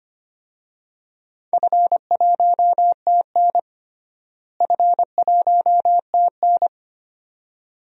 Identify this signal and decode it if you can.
Morse code, heard as a keyed tone, about 25 wpm, 700 Hz. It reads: F1TN F1TN
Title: DXCC-100 Call Signs - 4 Characters - Repeated 2x - 25wpm